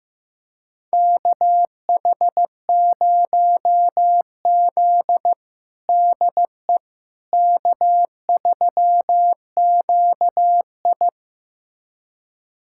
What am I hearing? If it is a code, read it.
Morse code, heard as a keyed tone, about 15 wpm, 700 Hz. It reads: KH0Z DE K3QI
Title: USA Callsign DE Callsign - 4 Characters - 15wpm